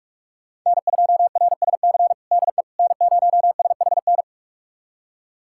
Morse code, heard as a keyed tone, about 46 wpm, 700 Hz. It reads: N2RSC DE N0SHN
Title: USA Callsign DE Callsign - 5 Characters - 45wpm